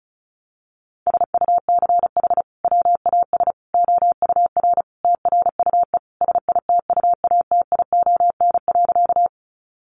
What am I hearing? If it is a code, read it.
Morse code, heard as a keyed tone, about 34 wpm, 700 Hz. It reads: SUCH WAS OUR TRUE SITUATION.
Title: Sentences from Top 1200 Words 35wpm